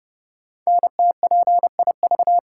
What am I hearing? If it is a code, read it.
Morse code, heard as a keyed tone, about 30 wpm, 700 Hz. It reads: NTPIV
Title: Code Groups - 5 Letter:Numbers - 30wpm